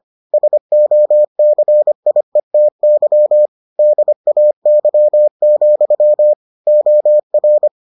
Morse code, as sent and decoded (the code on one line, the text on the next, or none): ... --- -.-. .. - -.-- / -.. .- -.-- --..-- / --- .-.
SOCIETY DAY, OR